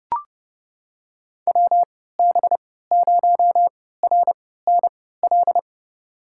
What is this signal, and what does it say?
Morse code, heard as a keyed tone, about 31 wpm, spaced as for 17 wpm, 700 Hz. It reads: WB0RNL
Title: USA Call Signs - 6 Characters - 3x Character Spacing - 30wpm